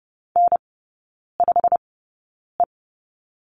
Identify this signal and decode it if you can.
Morse code, heard as a keyed tone, about 30 wpm, spaced as for 9 wpm, 700 Hz. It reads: N5E
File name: Lesson 18e - Letter m - Three Characters 30wpm